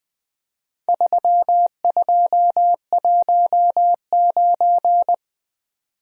Morse code, heard as a keyed tone, about 20 wpm, 700 Hz. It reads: 3219
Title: Four Letter-Number Combinations - 20wpm